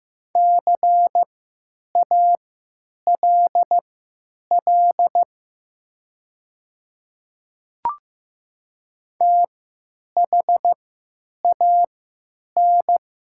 Morse code, heard as a keyed tone, about 15 wpm, spaced as for 9 wpm, 700 Hz. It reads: CALL THAN
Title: 3x Character Spacing - Top 100 Words - 15wpm